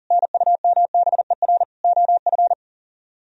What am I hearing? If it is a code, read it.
Morse code, heard as a keyed tone, about 40 wpm, 700 Hz. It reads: NUMBER OF